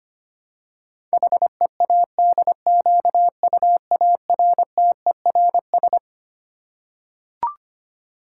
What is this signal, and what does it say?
Morse code, heard as a keyed tone, about 25 wpm, 700 Hz. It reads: HEADQUARTERS
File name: Twelve Letter Words - 25wpm